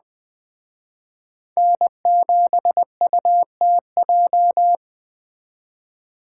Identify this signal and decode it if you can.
Morse code, heard as a keyed tone, about 21 wpm, 700 Hz. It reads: N7UTJ
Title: Speed Racing - USA 1x3 Call Signs 17wpm